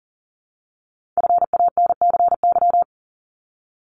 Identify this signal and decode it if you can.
Morse code, heard as a keyed tone, about 38 wpm, 700 Hz. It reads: FANCY
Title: Multi-letter Phonemes 40wpm